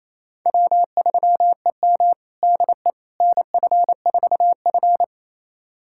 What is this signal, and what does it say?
Morse code, heard as a keyed tone, about 28 wpm, 700 Hz. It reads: W3EM DE NF4F
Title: USA Callsign DE Callsign - 4 Characters - 28wpm